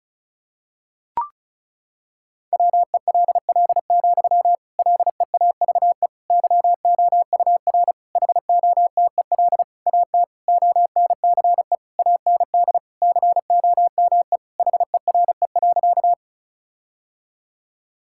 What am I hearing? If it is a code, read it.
Morse code, heard as a keyed tone, about 34 wpm, 700 Hz. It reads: WELL, LEAVE YOUR HOTEL AT ONCE AND COME HERE.
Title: Sentences from Top 1600 Words 35wpm